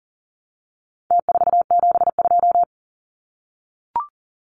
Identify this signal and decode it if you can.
Morse code, heard as a keyed tone, about 40 wpm, 700 Hz. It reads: T472
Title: Numbers - Contest Style - 4-Digit 40wpm